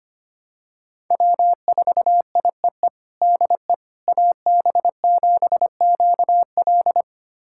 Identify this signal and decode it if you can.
Morse code, heard as a keyed tone, about 25 wpm, 700 Hz. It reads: W4IEE DE AB7QL
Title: USA Callsign DE Callsign - 5 Characters - 25wpm